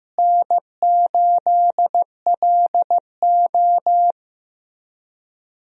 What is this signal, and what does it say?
Morse code, heard as a keyed tone, about 15 wpm, 700 Hz. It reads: N8LO
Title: USA Call Signs - 4 Characters - Repeated 2x - 15wpm